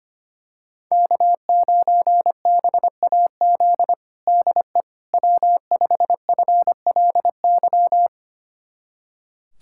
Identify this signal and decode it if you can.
Morse code, heard as a keyed tone, about 25 wpm, 700 Hz. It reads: K9BAZ DE W5FLY
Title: USA Callsign DE Callsign - 5 Characters - 25wpm